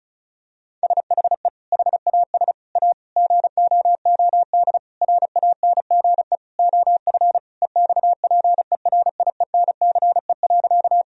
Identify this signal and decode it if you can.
Morse code, heard as a keyed tone, about 36 wpm, 700 Hz. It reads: SHE HAS A GOOD RANGE OF EXPERIENCE.
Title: Sentences from Top 1000 Words - Review Part 4 - 35wpm